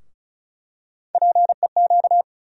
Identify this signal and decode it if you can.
Morse code, heard as a keyed tone, about 36 wpm, 700 Hz. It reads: PEQ
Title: USA Call Sign Postfixes - 3 Letters 35wpm